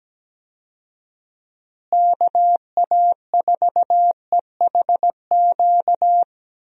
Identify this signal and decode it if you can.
Morse code, heard as a keyed tone, about 17 wpm, 700 Hz. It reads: KA4EHQ
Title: USA Call Signs - 6 Characters - Repeated 3x - 17wpm